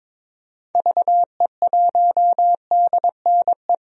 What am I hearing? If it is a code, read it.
Morse code, heard as a keyed tone, about 21 wpm, 700 Hz. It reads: VE1DNE
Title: DXCC-100 Call Signs - 6 Characters - Repeated 2x - 22wpm